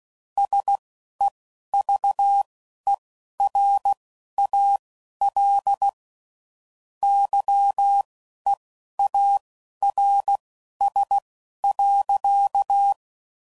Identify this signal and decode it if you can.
Morse code, heard as a keyed tone, about 16 wpm, spaced as for 11 wpm, 800 Hz. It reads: SEVERAL YEARS.
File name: CW Code Groups @10WPM / QSOs / QSO kc7haa